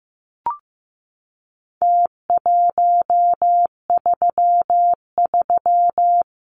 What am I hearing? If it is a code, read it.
Morse code, heard as a keyed tone, about 15 wpm, 700 Hz. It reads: T133
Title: Numbers - Contest Style - 4-Digit 15wpm